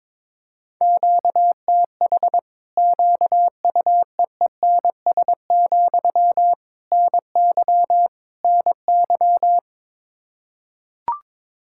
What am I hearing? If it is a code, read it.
Morse code, heard as a keyed tone, about 22 wpm, 700 Hz. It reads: QTH QUEENS, NY NY
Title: Practice QSO - Exchange 1 - Beginning - 22wpm